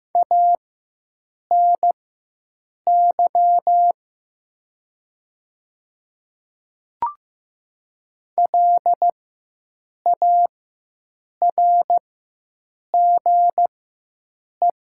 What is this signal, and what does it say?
Morse code, heard as a keyed tone, about 15 wpm, spaced as for 7 wpm, 700 Hz. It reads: ANY LARGE
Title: 4x Character Spacing - Top 200 Words - 15wpm